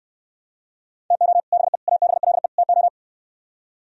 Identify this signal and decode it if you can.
Morse code, heard as a keyed tone, about 75 wpm, 700 Hz. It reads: TO BE ABLE TO